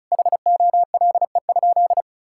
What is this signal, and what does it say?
Morse code, heard as a keyed tone, about 36 wpm, 700 Hz. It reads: HOLE?
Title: Sentences from Top 1600 Words 35wpm